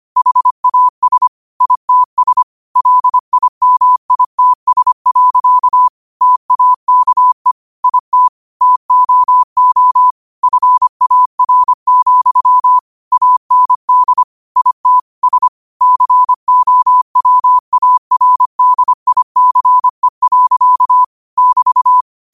Morse code, heard as a keyed tone, about 25 wpm, 1000 Hz. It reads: HAS ITS LIMITS. TAKE IT TOO FAR, AND IT S COWARDICE. <BT>
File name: qotdfiles / 25wpm